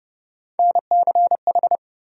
Morse code, heard as a keyed tone, about 30 wpm, 700 Hz. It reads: NCH
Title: USA Call Sign Postfixes - 3 Letters 30wpm